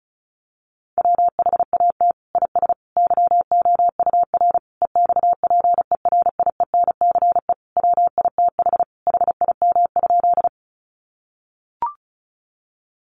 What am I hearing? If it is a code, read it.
Morse code, heard as a keyed tone, about 34 wpm, 700 Hz. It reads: WHAT IS YOUR EXPERIENCE WITH HIM?
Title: Sentences from Top 600 Words 35wpm